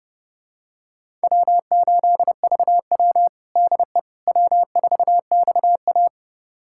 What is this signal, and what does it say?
Morse code, heard as a keyed tone, about 32 wpm, 700 Hz. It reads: W8VW DE W4XA
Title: USA Callsign DE Callsign - 4 Characters - 30wpm